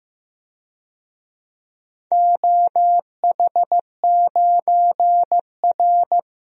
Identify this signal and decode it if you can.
Morse code, heard as a keyed tone, about 15 wpm, 700 Hz. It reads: OH9R
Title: DXCC-100 Call Signs - 4 Characters - Repeated 3x - 15wpm